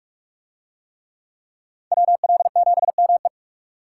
Morse code, heard as a keyed tone, about 39 wpm, 700 Hz. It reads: WL7ME